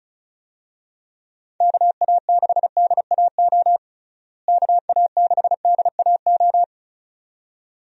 Morse code, heard as a keyed tone, about 36 wpm, 700 Hz. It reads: KA6DAO KA6DAO
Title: USA Call Signs - 6 Characters - Repeated 2x - 35wpm